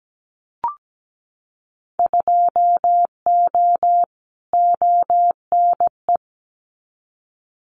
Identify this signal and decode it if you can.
Morse code, heard as a keyed tone, about 17 wpm, 700 Hz. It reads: 2O ONE
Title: Winter Field Day Exchanges Encore 17wpm